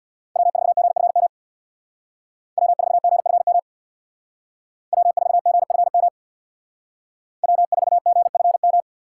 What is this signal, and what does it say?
Morse code, heard as a keyed tone, about 64 wpm, 700 Hz. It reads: W4GFM W4GFM W4GFM W4GFM